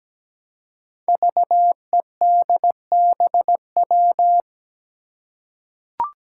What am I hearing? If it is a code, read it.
Morse code, heard as a keyed tone, about 17 wpm, 700 Hz. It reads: VEDBW
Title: Five Challenging Letter-Number Combinations 17wpm